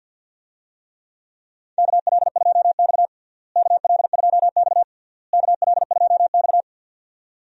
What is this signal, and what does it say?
Morse code, heard as a keyed tone, about 52 wpm, 700 Hz. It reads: KL2X KL2X KL2X
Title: USA Call Signs - 4 Characters - Repeated 3x - 50wpm